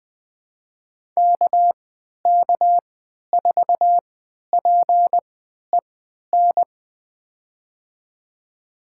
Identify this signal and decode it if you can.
Morse code, heard as a keyed tone, about 20 wpm, spaced as for 11 wpm, 700 Hz. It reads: KK4PEN
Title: USA Call Signs - 6 Characters - 3x Character Spacing - 20wpm